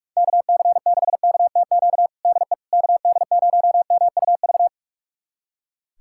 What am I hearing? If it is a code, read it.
Morse code, heard as a keyed tone, about 46 wpm, 700 Hz. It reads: KC6KTQ DE KD0MUV
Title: USA Callsign DE Callsign 45wpm